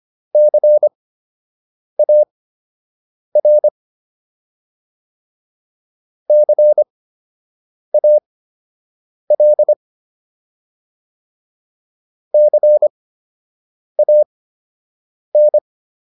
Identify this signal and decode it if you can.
Morse code, heard as a keyed tone, about 25 wpm, spaced as for 7 wpm, 600 Hz. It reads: CAR CAL CAN